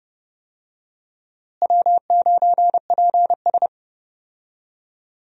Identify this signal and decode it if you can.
Morse code, heard as a keyed tone, about 30 wpm, 700 Hz. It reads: W9PS
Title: USA Call Signs - 4 Characters - Repeated 3x - 30wpm